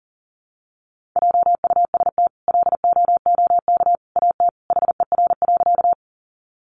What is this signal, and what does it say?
Morse code, heard as a keyed tone, about 40 wpm, 700 Hz. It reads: JUST LOOK AT HER.
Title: Sentences from Top 200 Words 40wpm